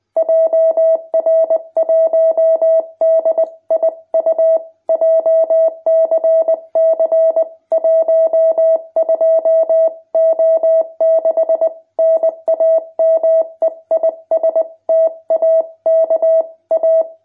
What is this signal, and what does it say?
Morse code, heard as a keyed tone, about 20 wpm, 660 Hz. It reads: JR1DIU JCC 12O6 NAME IS TAKA
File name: jh8cbh / 2021cw